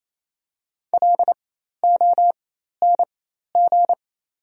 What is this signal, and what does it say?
Morse code, heard as a keyed tone, about 28 wpm, spaced as for 13 wpm, 690 Hz. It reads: LONG
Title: 4x Character Spacing - Top 200 Words - 28wpm